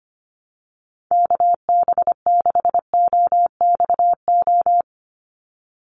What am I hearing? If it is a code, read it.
Morse code, heard as a keyed tone, about 25 wpm, 700 Hz. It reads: KB6OXO
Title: USA Call Signs 25wpm